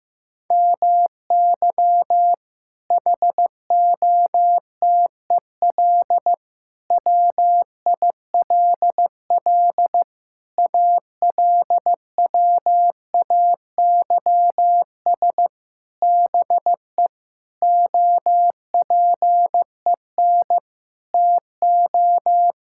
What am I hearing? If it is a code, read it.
Morse code, heard as a keyed tone, about 15 wpm, 700 Hz. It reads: MY HOTEL WILL ALWAYS BE OPEN TO
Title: Sentences from Top 1600 Words 15wpm